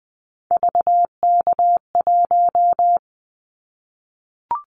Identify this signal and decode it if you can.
Morse code, heard as a keyed tone, about 19 wpm, 700 Hz. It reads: VK1
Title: Callsign Prefixes for Top 50 DXCC - 20wpm